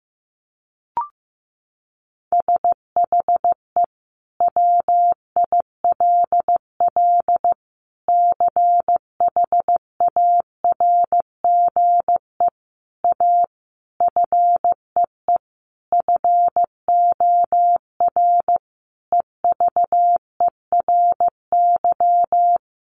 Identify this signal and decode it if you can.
Morse code, heard as a keyed tone, about 15 wpm, 700 Hz. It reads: SHE WILL CHARGE A FEE FOR EVERY
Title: Sentences from Top 1500 Words 15wpm